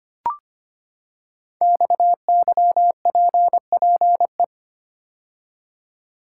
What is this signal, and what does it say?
Morse code, heard as a keyed tone, about 25 wpm, 700 Hz. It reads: XYPPE
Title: Code Groups - 5 Letters - 25wpm